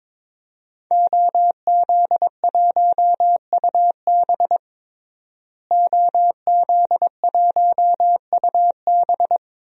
Morse code, heard as a keyed tone, about 22 wpm, 700 Hz. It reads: OZ1UB OZ1UB
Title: DXCC-100 Call Signs - 5 Characters - Repeated 2x - 22wpm